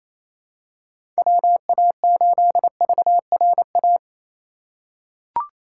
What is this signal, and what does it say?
Morse code, heard as a keyed tone, about 28 wpm, 700 Hz. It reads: WA8VRA
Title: USA Call Signs - 6 Characters - 2x Character Spacing - 28wpm